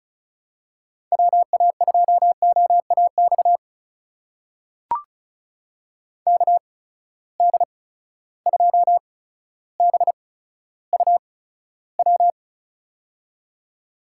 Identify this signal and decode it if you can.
Morse code, heard as a keyed tone, about 34 wpm, spaced as for 15 wpm, 700 Hz. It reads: WA2OAX KD2BUW
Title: USA Call Signs - 6 Characters - 8x Character Spacing - 35wpm